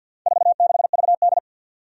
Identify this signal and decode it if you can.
Morse code, heard as a keyed tone, about 53 wpm, 700 Hz. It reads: 46VD
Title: Four Challenging Letter-Number Combinations 50wpm